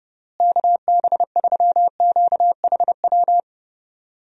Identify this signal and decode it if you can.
Morse code, heard as a keyed tone, about 30 wpm, 700 Hz. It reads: KB3QHW
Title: USA Call Signs - 6 Characters - Repeated 2x - 30wpm